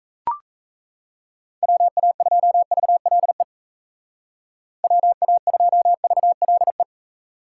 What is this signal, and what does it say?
Morse code, heard as a keyed tone, about 33 wpm, 700 Hz. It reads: WA2VLE WA2VLE
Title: Speed Racing - USA 2x3 Call Signs 28wpm